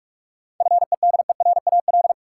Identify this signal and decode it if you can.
Morse code, heard as a keyed tone, about 46 wpm, 700 Hz. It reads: FEDERAL
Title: Sets of 3 Words - New Word Drill - 45wpm